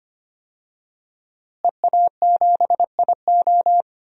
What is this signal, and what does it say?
Morse code, heard as a keyed tone, about 25 wpm, 700 Hz. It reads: EA7IO
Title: DXCC-100 Call Signs - 5 Characters - Repeated 3x - 25wpm